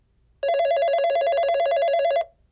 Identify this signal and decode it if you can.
There is no code.